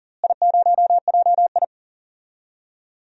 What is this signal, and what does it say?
Morse code, heard as a keyed tone, about 40 wpm, 700 Hz. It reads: I0JI
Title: DXCC-100 Call Signs - 4 Characters - Repeated 3x - 40wpm